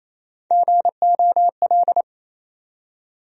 Morse code, heard as a keyed tone, about 27 wpm, 700 Hz. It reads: GOL
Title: USA Call Sign Postfixes - 3 Letters 28wpm